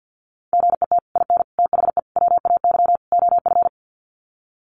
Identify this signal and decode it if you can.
Morse code, heard as a keyed tone, about 49 wpm, 700 Hz. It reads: GET IN THE WAY OF
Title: Sets of 5 Words Encore 50wpm